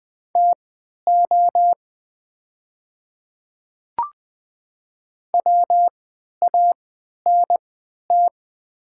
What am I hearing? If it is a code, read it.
Morse code, heard as a keyed tone, about 20 wpm, spaced as for 11 wpm, 690 Hz. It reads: TO WANT
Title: 3x Character Spacing - Top 200 Words - 20wpm